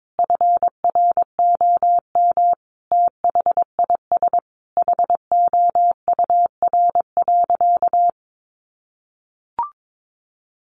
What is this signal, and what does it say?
Morse code, heard as a keyed tone, about 22 wpm, 700 Hz. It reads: FROM THIS HOUR.
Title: Sentences from Top 1100 Words 22wpm